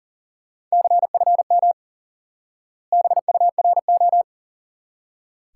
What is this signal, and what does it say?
Morse code, heard as a keyed tone, about 37 wpm, 700 Hz. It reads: CFM BURO
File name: IQE Course 3 - Lesson 89 - YAGI - Two Elements - 40wpm